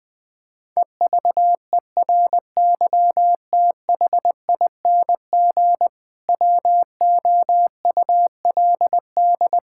Morse code, heard as a keyed tone, about 20 wpm, 700 Hz. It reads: EVERYTHING WOULD
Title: Sentences from Top 1000 Words - Review Part 5 - 20wpm